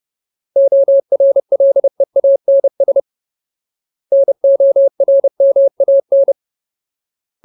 Morse code, heard as a keyed tone, about 30 wpm, 550 Hz. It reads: ORLEANS NORMAN
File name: Ultimate Word List / 30WPM Ultimate